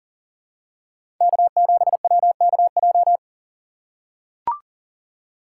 Morse code, heard as a keyed tone, about 40 wpm, 700 Hz. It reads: K7WKJ
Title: USA Call Signs - 5 Characters - 8x Character Spacing - 40wpm